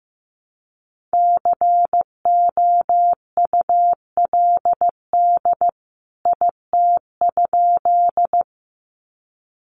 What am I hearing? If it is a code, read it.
Morse code, heard as a keyed tone, about 15 wpm, 700 Hz. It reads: COULD IT?
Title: A Sound Of Thunder 15wpm